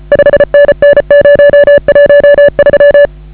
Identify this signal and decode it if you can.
Morse code, heard as a keyed tone, about 34 wpm, 560 Hz. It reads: 5NN 013